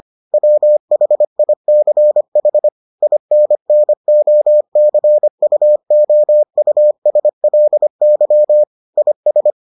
Morse code, heard as a keyed tone, about 25 wpm, 600 Hz. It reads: WHICH INNOCUOUSLY IS